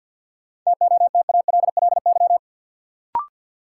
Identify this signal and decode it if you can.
Morse code, heard as a keyed tone, about 53 wpm, 700 Hz. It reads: TOTALLY
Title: Sentences from Top 1600 Words - New Word Drill Alt - 50wpm